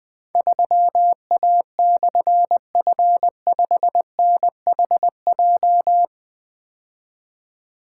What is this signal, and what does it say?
Morse code, heard as a keyed tone, about 20 wpm, 700 Hz. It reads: 3A/F5NHJ
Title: Difficult Call Signs 20wpm